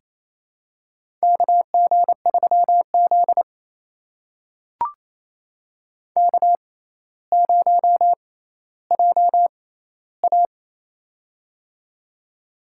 Morse code, heard as a keyed tone, about 28 wpm, spaced as for 14 wpm, 700 Hz. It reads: KG3Z K0JA
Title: USA Call Signs - 4 Characters - 6x Character Spacing - 28wpm